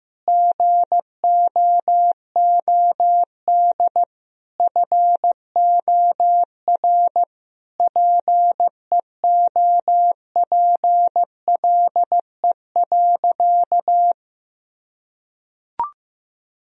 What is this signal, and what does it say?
Morse code, heard as a keyed tone, about 15 wpm, 700 Hz. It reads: GOOD FOR PEOPLE.
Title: Sentences from Top 1000 Words - Review Part 3 - 15wpm